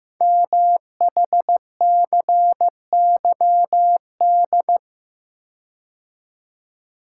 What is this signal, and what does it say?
Morse code, heard as a keyed tone, about 15 wpm, 700 Hz. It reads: MHCYD
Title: Code Groups - 5 Letters - 15wpm